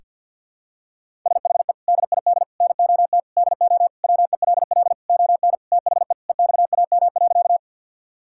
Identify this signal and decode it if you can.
Morse code, heard as a keyed tone, about 52 wpm, 700 Hz. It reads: SHE DID NOT DO WELL ON THE EXAM.
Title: Sentences from Top 1000 Words - Review Part 5 - 50wpm